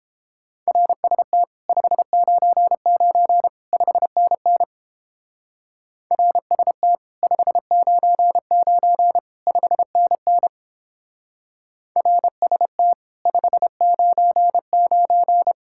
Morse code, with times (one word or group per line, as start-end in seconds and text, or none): 0.7-1.4 RST
1.7-3.5 599
3.7-4.6 5NN
6.1-7.0 RST
7.2-9.2 599
9.5-10.5 5NN
12.0-12.9 RST
13.3-15.5 599